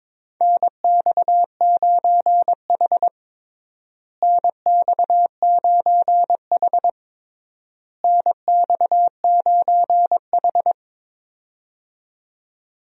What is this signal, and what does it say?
Morse code, heard as a keyed tone, about 21 wpm, 700 Hz. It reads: NX9H NX9H NX9H
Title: USA Call Signs - 4 Characters - Repeated 3x - 22wpm